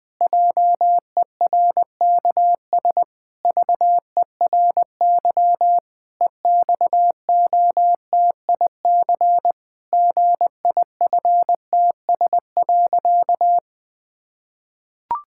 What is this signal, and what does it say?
Morse code, heard as a keyed tone, about 20 wpm, 700 Hz. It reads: JERKS VERY EXOTIC GIFTS.
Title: Pangram Challenge v2 - 20wpm